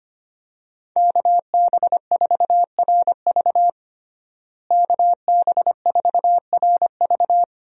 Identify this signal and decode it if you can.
Morse code, heard as a keyed tone, about 25 wpm, 700 Hz. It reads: KB4RV KB4RV
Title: USA Call Signs - 5 Characters - Repeated 3x - 25wpm